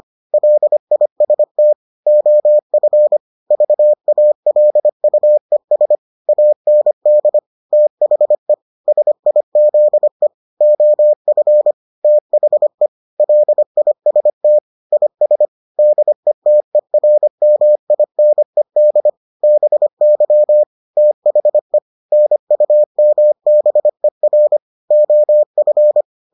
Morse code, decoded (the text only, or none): LIST OF VALUES AND THE SIZE OF THE LIST IS DETERMINED BY THE NUMBER OF